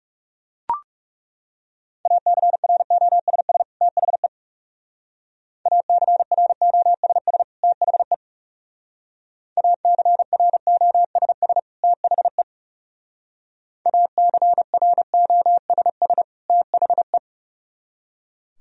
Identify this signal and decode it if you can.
Morse code, heard as a keyed tone, about 37 wpm, 700 Hz. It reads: ACROSS THE ACROSS THE ACROSS THE ACROSS THE
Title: Speed-Racing - Sets of 2 Words 30wpm